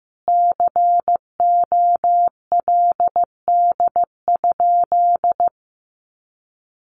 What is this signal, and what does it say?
Morse code, heard as a keyed tone, about 15 wpm, 700 Hz. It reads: COLD?
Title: Sentences from Top 1400 Words 15wpm